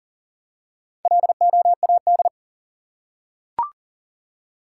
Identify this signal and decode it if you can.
Morse code, heard as a keyed tone, about 40 wpm, 700 Hz. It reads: LOAD
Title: Four Letter Words - 40wpm